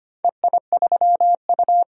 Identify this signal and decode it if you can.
Morse code, heard as a keyed tone, about 25 wpm, 700 Hz. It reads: EI3U
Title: DXCC-100 Call Signs - 4 Characters - 25wpm